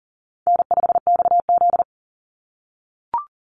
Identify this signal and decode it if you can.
Morse code, heard as a keyed tone, about 38 wpm, 700 Hz. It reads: N5XZ
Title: Easy Call Signs - 40wpm